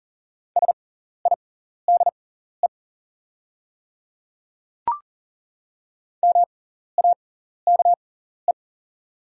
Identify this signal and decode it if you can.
Morse code, heard as a keyed tone, about 40 wpm, spaced as for 14 wpm, 690 Hz. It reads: SIDE MAKE
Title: 6x Character Spacing - Top 100 Words - 40wpm